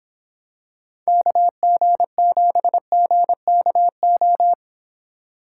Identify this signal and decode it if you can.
Morse code, heard as a keyed tone, about 24 wpm, 700 Hz. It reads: KG7GKO